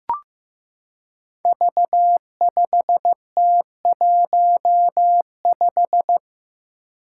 Morse code, heard as a keyed tone, about 15 wpm, 700 Hz. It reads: V5T15
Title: Five Challenging Letter-Number Combinations 15wpm